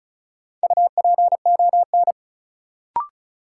Morse code, heard as a keyed tone, about 36 wpm, 700 Hz. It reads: UPON
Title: Sentences from Top 1000 Words - Review Part 6 - New Word Drill - 35wpm